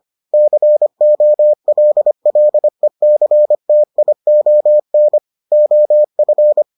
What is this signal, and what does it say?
Morse code, heard as a keyed tone, about 25 wpm, 600 Hz. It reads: COLLECTION OF